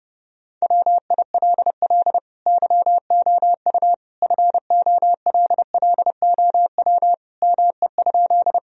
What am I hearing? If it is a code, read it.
Morse code, heard as a keyed tone, about 32 wpm, 700 Hz. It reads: WILL YOU FOLLOW ME?
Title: Sentences from Top 1000 Words - Review Part 1 - 30wpm